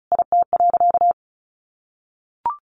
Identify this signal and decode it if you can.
Morse code, heard as a keyed tone, about 34 wpm, 700 Hz. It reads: IT.
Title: Sentences from Top 1200 Words 35wpm